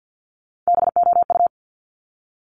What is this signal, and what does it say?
Morse code, heard as a keyed tone, about 50 wpm, 700 Hz. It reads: BOU
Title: Top Three Letter Combinations in Words 50wpm